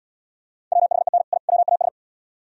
Morse code, heard as a keyed tone, about 74 wpm, 700 Hz. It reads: WHEN I WAS